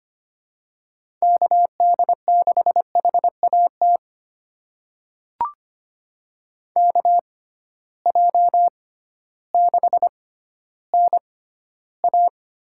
Morse code, heard as a keyed tone, about 25 wpm, spaced as for 13 wpm, 700 Hz. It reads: KD6HAT KJ6NA
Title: USA Call Signs - 6 Characters - 6x Character Spacing - 25wpm